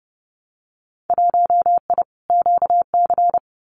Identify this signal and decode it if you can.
Morse code, heard as a keyed tone, about 30 wpm, 700 Hz. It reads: 1I QC
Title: Winter Field Day Exchanges 30wpm